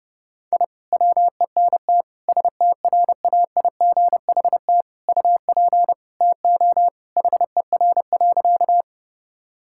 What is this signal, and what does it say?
Morse code, heard as a keyed tone, about 32 wpm, 700 Hz. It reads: I WENT STRAIGHT UP TO HER.
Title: Sentences from Top 1000 Words - Review Part 6 - 30wpm